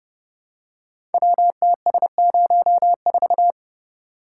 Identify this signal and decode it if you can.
Morse code, heard as a keyed tone, about 28 wpm, 700 Hz. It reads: WTS04
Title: Code Groups - 5 Letter:Numbers - 30wpm